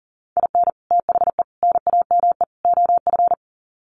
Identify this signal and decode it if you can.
Morse code, heard as a keyed tone, about 40 wpm, 700 Hz. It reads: IN THE NAME OF
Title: Sets of 4 Words 40wpm